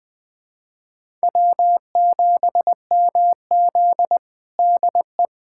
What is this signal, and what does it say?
Morse code, heard as a keyed tone, about 21 wpm, 700 Hz. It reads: W7MZ DE
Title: USA Callsign DE Callsign - 4 Characters - 20wpm